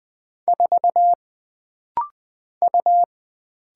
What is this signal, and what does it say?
Morse code, heard as a keyed tone, about 20 wpm, 700 Hz. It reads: 4 U
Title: Single Letter-Number Alt - Rapid-Fire 20wpm